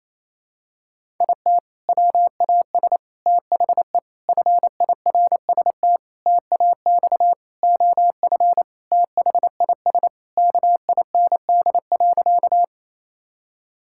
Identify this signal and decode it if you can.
Morse code, heard as a keyed tone, about 28 wpm, 700 Hz. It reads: IT WAS THE FIRST TAX OF THIS KIND.
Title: Sentences from Top 1000 Words - Review Part 5 - 28wpm